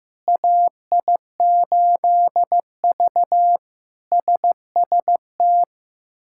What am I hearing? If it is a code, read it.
Morse code, heard as a keyed tone, about 15 wpm, 700 Hz. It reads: AI8V SST
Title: K1USN SST QSO Practice - 15wpm